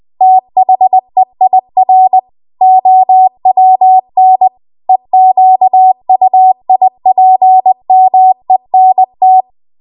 Morse code, heard as a keyed tone, about 20 wpm, 750 Hz. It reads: THEIR OWN EQUIPMENT